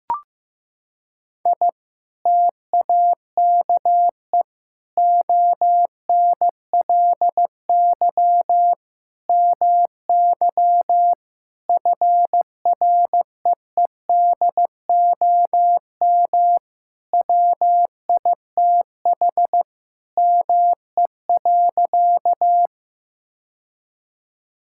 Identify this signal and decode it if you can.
Morse code, heard as a keyed tone, about 15 wpm, 700 Hz. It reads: I TAKE ONLY MY FREEDOM WITH ME.
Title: Sentences from Top 1000 Words - Review Part 5 - 15wpm